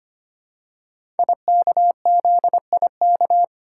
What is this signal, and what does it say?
Morse code, heard as a keyed tone, about 25 wpm, 700 Hz. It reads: IKZIK
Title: Code Groups - 5 Letters - 25wpm